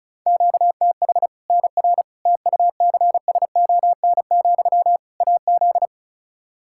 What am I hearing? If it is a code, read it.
Morse code, heard as a keyed tone, about 36 wpm, 700 Hz. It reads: QTH NR TUCSON, AZ
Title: K1USN SST QSO Practice 35wpm